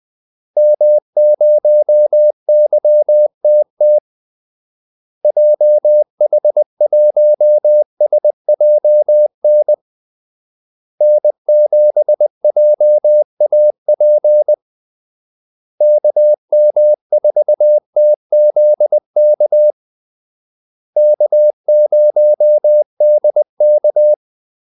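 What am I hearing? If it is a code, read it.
Morse code, heard as a keyed tone, about 20 wpm, 600 Hz. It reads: M0YTT JH1SJN N7JAP KM4TZK K0DK
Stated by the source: Morse call-sign practice at 20 wpm.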